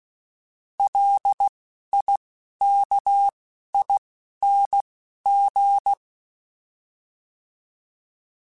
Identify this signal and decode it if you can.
Morse code, heard as a keyed tone, about 16 wpm, 800 Hz. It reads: L I K I N G